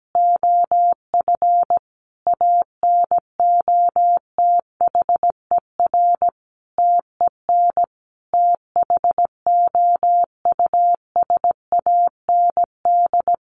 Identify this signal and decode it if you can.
Morse code, heard as a keyed tone, about 17 wpm, 700 Hz. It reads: OF ANOTHER TEN THOUSAND
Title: A Sound Of Thunder 17wpm